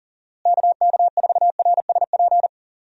Morse code, heard as a keyed tone, about 40 wpm, 700 Hz. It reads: KK4RSP
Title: USA Call Signs - 6 Characters - Repeated 2x - 40wpm